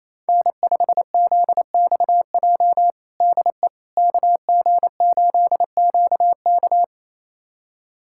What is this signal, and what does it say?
Morse code, heard as a keyed tone, about 28 wpm, 700 Hz. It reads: N5ZXJ DE KG8QK
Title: USA Callsign DE Callsign - 5 Characters - 28wpm